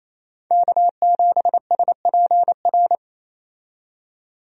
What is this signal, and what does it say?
Morse code, heard as a keyed tone, about 28 wpm, 700 Hz. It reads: K7SPR
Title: USA Call Signs - 5 Characters - 28wpm